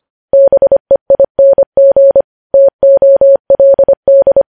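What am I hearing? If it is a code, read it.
Morse code, heard as a keyed tone, about 25 wpm, 550 Hz. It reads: BEING TOLD
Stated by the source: Morse code news headlines at 25 words per minute.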